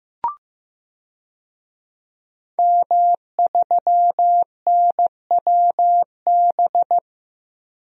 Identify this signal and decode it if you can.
Morse code, heard as a keyed tone, about 16 wpm, 700 Hz. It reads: M3NWB
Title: DXCC-100 Call Signs - 5 Characters - Repeated 3x - 15wpm